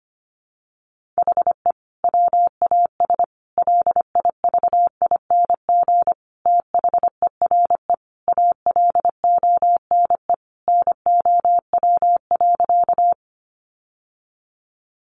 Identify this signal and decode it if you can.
Morse code, heard as a keyed tone, about 25 wpm, 700 Hz. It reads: HE WAS LIVING THERE ALONE NOW.
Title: Sentences from Top 1100 Words 25wpm